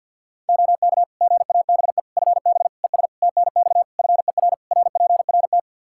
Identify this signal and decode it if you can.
Morse code, heard as a keyed tone, about 53 wpm, 700 Hz. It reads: OK GABE FB ES TNX FER RPRT
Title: Practice QSO - Exchange 2 - Info - 50wpm